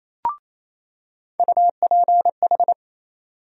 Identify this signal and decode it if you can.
Morse code, heard as a keyed tone, about 28 wpm, 700 Hz. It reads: UPH
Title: USA Call Sign Postfixes - 3 Letters 28wpm